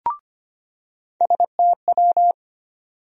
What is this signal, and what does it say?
Morse code, heard as a keyed tone, about 25 wpm, 700 Hz. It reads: STW
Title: USA Call Sign Postfixes - 3 Letters 25wpm